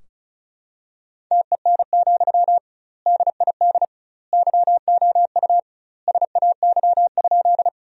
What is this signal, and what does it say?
Morse code, heard as a keyed tone, about 36 wpm, 700 Hz. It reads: TEN, DID YOU SAY?
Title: 2x Word-Spacing - Sentences from Top 400 Words - 35wpm